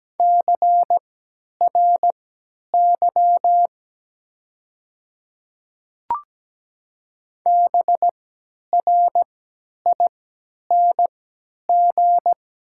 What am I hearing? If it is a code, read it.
Morse code, heard as a keyed tone, about 17 wpm, spaced as for 10 wpm, 700 Hz. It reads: CRY BRING
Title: 3x Character Spacing - Top 500 Words - 17wpm